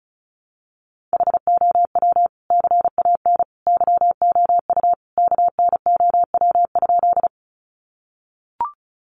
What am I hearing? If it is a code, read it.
Morse code, heard as a keyed tone, about 34 wpm, 700 Hz. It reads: HOW CAN YOU KNOW?
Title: Sentences from Top 100 Words 35wpm